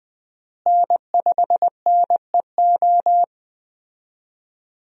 Morse code, heard as a keyed tone, about 20 wpm, 700 Hz. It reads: N5NEO
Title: USA Call Signs - 5 Characters - Repeated 2x - 20wpm